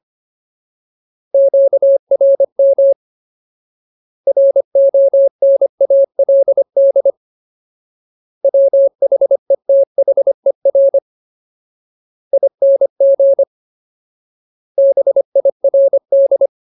Morse code, heard as a keyed tone, about 25 wpm, 550 Hz. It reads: QRM RONALD WHETHER ING BIRD
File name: Ultimate Word List / 25WPM Ultimate